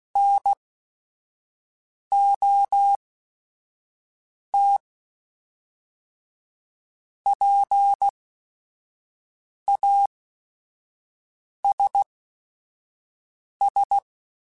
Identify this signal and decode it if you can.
Morse code, heard as a keyed tone, about 16 wpm, spaced as for 5 wpm, 800 Hz. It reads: NOT PASS
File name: Technical Subjects / Technical Subjects-mp3s / Sw Pwr Sup @5